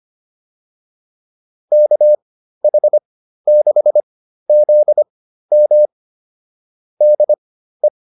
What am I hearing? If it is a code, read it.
Morse code, heard as a keyed tone, about 25 wpm, spaced as for 13 wpm, 600 Hz. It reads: KH6ZM DE
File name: QSO109 13